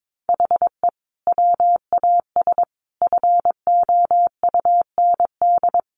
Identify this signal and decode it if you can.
Morse code, heard as a keyed tone, about 22 wpm, 700 Hz. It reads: HE WAS FOUND
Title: Sentences from Top 1500 Words 22wpm